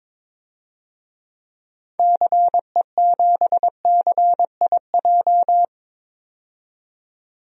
Morse code, heard as a keyed tone, about 22 wpm, 700 Hz. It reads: CE7CIJ
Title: DXCC-100 Call Signs - 6 Characters - 22wpm